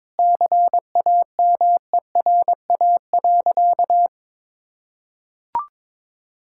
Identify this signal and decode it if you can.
Morse code, heard as a keyed tone, about 22 wpm, 700 Hz. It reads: CAMERA.
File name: practice / 22wpm / Sentences with 3-Word Phrases - 22wpm